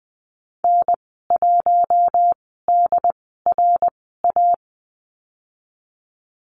Morse code, heard as a keyed tone, about 20 wpm, spaced as for 15 wpm, 700 Hz. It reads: N1DRA
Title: 122 Lesson 20g - Number 3 - Call Signs 20wpm